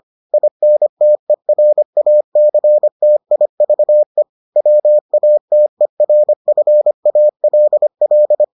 Morse code, heard as a keyed tone, about 25 wpm, 600 Hz. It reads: INTERACTIVE WATERFALL